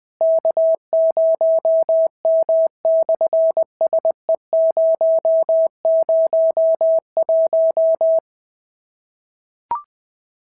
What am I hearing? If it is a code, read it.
Morse code, heard as a keyed tone, about 20 wpm, 650 Hz. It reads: K0M/SE001
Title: SOTA QSO Practice 20wpm